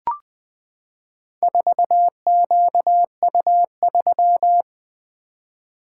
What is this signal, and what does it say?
Morse code, heard as a keyed tone, about 20 wpm, 700 Hz. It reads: 4QU3
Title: Four Letter-Number Combinations - 20wpm